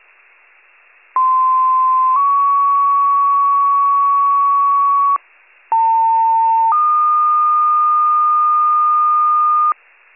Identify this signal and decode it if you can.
Two-tone page, heard as a one-second tone, then a three-second tone, about 1040 then 1120 Hz, about 900 then 1220 Hz. It is heard from 1.2 to 9.7 s.
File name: wcalert